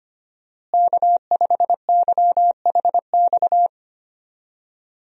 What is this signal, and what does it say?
Morse code, heard as a keyed tone, about 25 wpm, 700 Hz. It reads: K5YHX
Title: USA Call Signs - 5 Characters - Repeated 3x - 25wpm